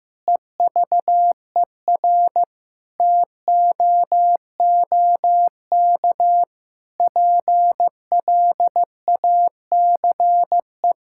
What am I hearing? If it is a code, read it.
Morse code, heard as a keyed tone, about 14 wpm, 700 Hz. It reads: EVER TOOK PLACE
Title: Sentences from Top 1000 Words - Review Part 6 - 15wpm